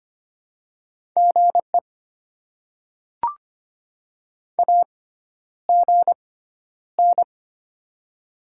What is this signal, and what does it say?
Morse code, heard as a keyed tone, about 25 wpm, spaced as for 11 wpm, 700 Hz. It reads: GE AGN
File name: Top 20 QSO Elements - 6x Character Spacing - 25wpm